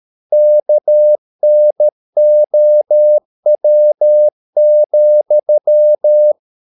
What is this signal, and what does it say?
Morse code, heard as a keyed tone, about 13 wpm, 600 Hz. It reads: KNOW,